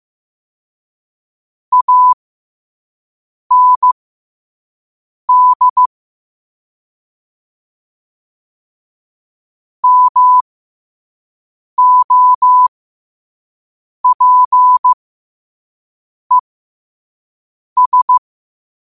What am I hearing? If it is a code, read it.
Morse code, heard as a keyed tone, about 15 wpm, spaced as for 5 wpm, 1000 Hz. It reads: AND MOPES
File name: qotdfiles / 5wpm